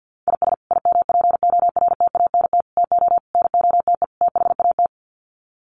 Morse code, heard as a keyed tone, about 50 wpm, 700 Hz. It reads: IS IMPORTANT TO NOTE THAT
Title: Sets of 5 Words 50wpm